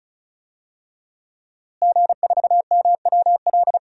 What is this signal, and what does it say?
Morse code, heard as a keyed tone, about 34 wpm, 700 Hz. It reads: G4MWL
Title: DXCC-100 Call Signs - 5 Characters - Repeated 2x - 35wpm